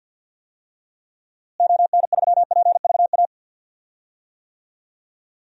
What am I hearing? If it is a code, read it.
Morse code, heard as a keyed tone, about 50 wpm, 700 Hz. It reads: ON3PVA
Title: DXCC-100 Call Signs - 6 Characters - Repeated 2x - 50wpm